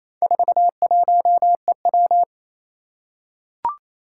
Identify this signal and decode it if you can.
Morse code, heard as a keyed tone, about 28 wpm, 700 Hz. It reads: 41EW
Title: Four Challenging Letter-Number Combinations 28wpm